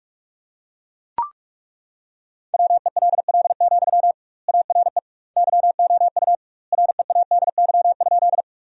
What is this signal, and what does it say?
Morse code, heard as a keyed tone, about 41 wpm, 700 Hz. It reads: WELL, ARE YOU READY?
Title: 2x Word-Spacing - Sentences from Top 400 Words - 45wpm